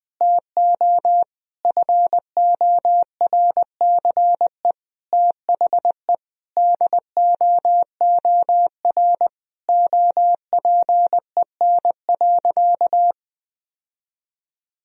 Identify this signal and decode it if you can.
Morse code, heard as a keyed tone, about 20 wpm, 700 Hz. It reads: TO FORCE THE DOOR OPEN.
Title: Sentences from Top 1000 Words - Review Part 3 - 20wpm